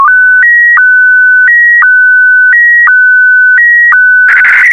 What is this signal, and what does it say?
SSTV (slow-scan television), Scottie DX.